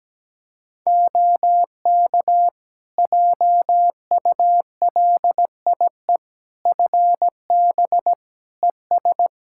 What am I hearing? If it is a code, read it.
Morse code, heard as a keyed tone, about 17 wpm, 700 Hz. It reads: OK JULIE FB ES
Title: Practice QSO - Exchange 2 - Info - 17wpm